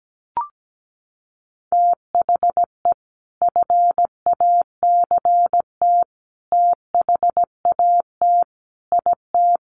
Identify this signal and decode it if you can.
Morse code, heard as a keyed tone, about 17 wpm, 700 Hz. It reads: THE FACT THAT IT
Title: Sets of 5 Words Encore2 17wpm